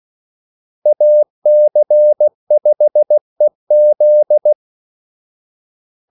Morse code, heard as a keyed tone, about 16 wpm, 600 Hz. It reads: AC5EZ
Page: Kaldesignaler 16 wpm